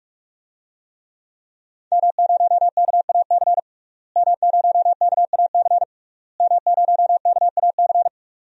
Morse code, heard as a keyed tone, about 41 wpm, 700 Hz. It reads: M0KAC M0KAC M0KAC
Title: DXCC-100 Call Signs - 5 Characters - Repeated 3x - 45wpm